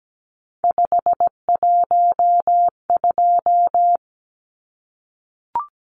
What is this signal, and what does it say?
Morse code, heard as a keyed tone, about 17 wpm, 700 Hz. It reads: 512
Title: Numbers - 3-Digit 17wpm